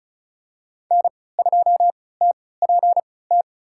Morse code, heard as a keyed tone, about 36 wpm, spaced as for 20 wpm, 700 Hz. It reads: N2TPT
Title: USA Call Signs - 5 Characters - 3x Character Spacing - 35wpm